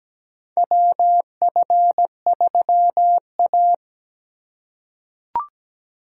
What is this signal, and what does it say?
Morse code, heard as a keyed tone, about 17 wpm, 700 Hz. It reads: WF3A
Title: USA Call Signs - 3x Character Spacing - 17wpm